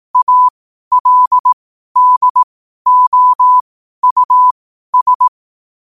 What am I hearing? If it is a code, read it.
Morse code, heard as a keyed tone, about 18 wpm, spaced as for 12 wpm, 1000 Hz. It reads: ALDOUS